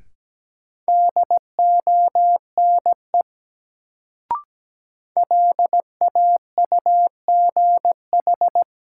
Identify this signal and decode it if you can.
Morse code, heard as a keyed tone, about 17 wpm, 700 Hz. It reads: DONE LAUGH
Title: Sentences from Top 500 Words - New Word Drill - 17wpm